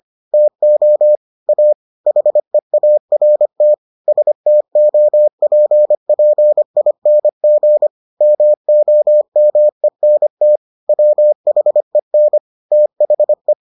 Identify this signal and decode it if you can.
Morse code, heard as a keyed tone, about 25 wpm, 600 Hz. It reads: TO A HEART STOPPING MOMENT WHEN THE